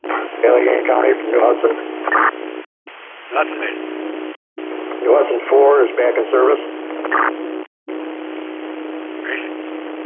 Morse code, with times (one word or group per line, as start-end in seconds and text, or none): none